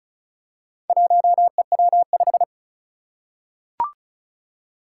Morse code, heard as a keyed tone, about 36 wpm, 700 Hz. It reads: 1EW5
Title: Four Challenging Letter-Number Combinations 35wpm